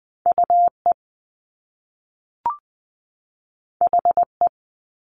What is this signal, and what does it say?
Morse code, heard as a keyed tone, about 20 wpm, 700 Hz. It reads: U E H E
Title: Top Two Letter Combinations in Words 20wpm